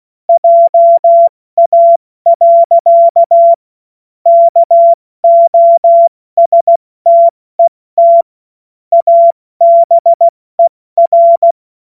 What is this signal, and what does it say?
Morse code, heard as a keyed tone, about 16 wpm, spaced as for 14 wpm, 680 Hz. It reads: JA. KOSTET ABER
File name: Morsekurs / Witze / Witz 05 März 2024 70 F80